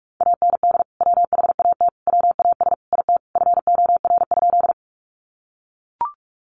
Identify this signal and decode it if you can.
Morse code, heard as a keyed tone, about 44 wpm, 700 Hz. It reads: AND WHAT WAS IT FOR?
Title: Sentences from Top 100 Words 45wpm